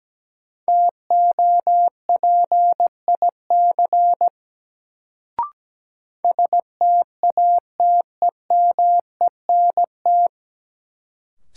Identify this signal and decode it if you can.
Morse code, heard as a keyed tone, about 17 wpm, 700 Hz. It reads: TOPIC STATEMENT
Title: Sentences from Top 1200 Words - New Word Drill - 17wpm